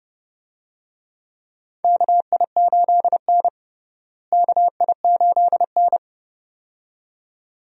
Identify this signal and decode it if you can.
Morse code, heard as a keyed tone, about 31 wpm, 700 Hz. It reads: KI8N KI8N
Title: USA Call Signs - 4 Characters - Repeated 2x - 30wpm